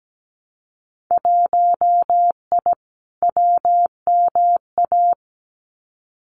Morse code, heard as a keyed tone, about 17 wpm, 700 Hz. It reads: 1I WMA
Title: Winter Field Day Exchanges Encore 17wpm